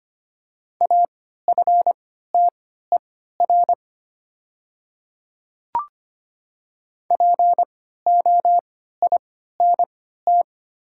Morse code, heard as a keyed tone, about 25 wpm, spaced as for 14 wpm, 700 Hz. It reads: AFTER POINT
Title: 3x Character Spacing - Top 300 Words - 25wpm